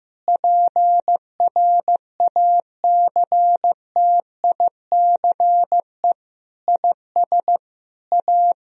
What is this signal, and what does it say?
Morse code, heard as a keyed tone, about 15 wpm, 700 Hz. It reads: PRACTICE IS A
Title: Sentences from Top 1000 Words - Review Part 4 - 15wpm